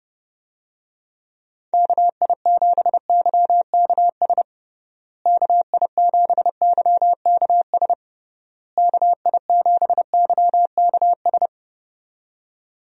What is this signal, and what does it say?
Morse code, heard as a keyed tone, about 30 wpm, 700 Hz. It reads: KI7YKS KI7YKS KI7YKS
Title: USA Call Signs - 6 Characters - Repeated 3x - 30wpm